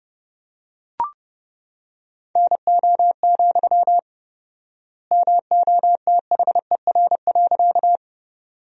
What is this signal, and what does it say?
Morse code, heard as a keyed tone, about 32 wpm, 700 Hz. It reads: NO, MOTHER.
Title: 4x Word-Spacing - Sentences from Top 200 Words - 30wpm